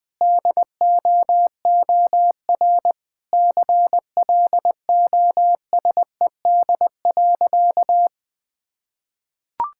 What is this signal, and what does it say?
Morse code, heard as a keyed tone, about 20 wpm, 700 Hz. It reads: DOOR CLOSED.
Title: Sentences with 3-Word Phrases - Spoken Beginning - 20wpm